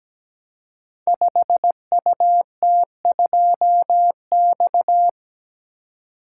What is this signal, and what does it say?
Morse code, heard as a keyed tone, about 17 wpm, 700 Hz. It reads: 5UT2X
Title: Code Groups - 5 Letter:Numbers - 17wpm